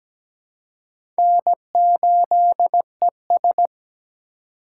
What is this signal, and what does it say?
Morse code, heard as a keyed tone, about 17 wpm, 700 Hz. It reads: N8ES
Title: USA Call Signs - 4 Characters - 2x Character Spacing - 17wpm